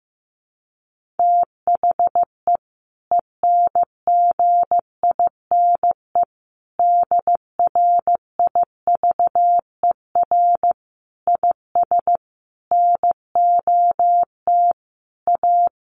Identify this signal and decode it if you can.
Morse code, heard as a keyed tone, about 15 wpm, 700 Hz. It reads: THE ENGINE DRIVER IS NOT A
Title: Sentences from Top 1500 Words 15wpm